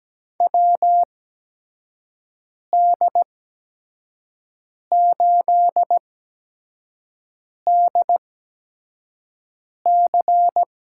Morse code, heard as a keyed tone, about 17 wpm, spaced as for 5 wpm, 700 Hz. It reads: WD8DC
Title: USA Call Signs - 6 Characters - 8x Character Spacing - 17wpm